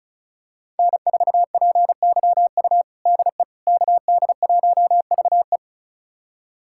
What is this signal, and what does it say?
Morse code, heard as a keyed tone, about 34 wpm, 700 Hz. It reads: N4PYU DE KD1VE
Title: USA Callsign DE Callsign - 5 Characters - 35wpm